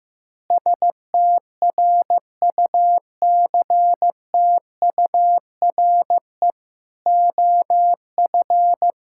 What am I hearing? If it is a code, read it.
Morse code, heard as a keyed tone, about 15 wpm, 700 Hz. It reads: STRUCTURE OF
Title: Sentences from Top 1000 Words - Review Part 6 - 15wpm